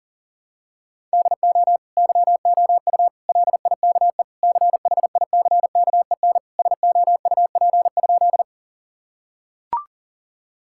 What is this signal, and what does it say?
Morse code, heard as a keyed tone, about 40 wpm, 700 Hz. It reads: DO YOU LIKE CHICKEN SOUP?
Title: Sentences from Top 1000 Words - Review Part 6 - 40wpm